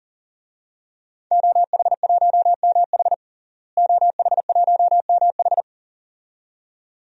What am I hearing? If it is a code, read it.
Morse code, heard as a keyed tone, about 37 wpm, 700 Hz. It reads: OH1MH OH1MH
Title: DXCC-100 Call Signs - 5 Characters - Repeated 2x - 40wpm